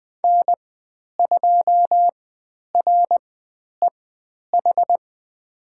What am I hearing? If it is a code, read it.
Morse code, heard as a keyed tone, about 20 wpm, spaced as for 10 wpm, 700 Hz. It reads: N2REH
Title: 102 Lesson 17g - Letter u - Call Signs 20wpm